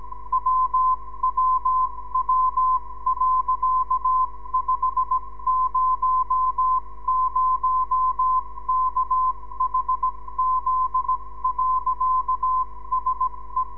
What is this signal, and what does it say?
Morse code, heard as a keyed tone, about 17 wpm, spaced as for 15 wpm, 1030 Hz. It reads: WWW.500KHZ.SE
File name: ndb zvok / sm6bgp